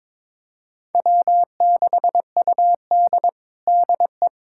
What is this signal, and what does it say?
Morse code, heard as a keyed tone, about 22 wpm, 700 Hz. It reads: W6UD DE
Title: USA Callsign DE Callsign - 4 Characters - 22wpm